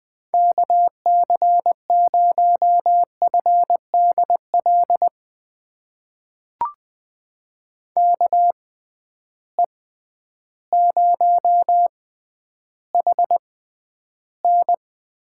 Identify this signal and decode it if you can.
Morse code, heard as a keyed tone, about 20 wpm, spaced as for 11 wpm, 700 Hz. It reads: KC0FDL KE0HN
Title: USA Call Signs - 6 Characters - 6x Character Spacing - 20wpm